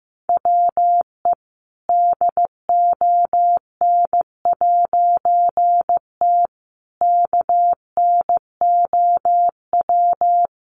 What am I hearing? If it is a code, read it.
Morse code, heard as a keyed tone, about 15 wpm, 700 Hz. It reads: WE DON'T KNOW
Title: A Sound Of Thunder 15wpm